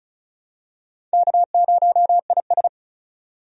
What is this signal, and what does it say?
Morse code, heard as a keyed tone, about 36 wpm, 700 Hz. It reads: K0IS
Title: USA Call Signs - 4 Characters - 35wpm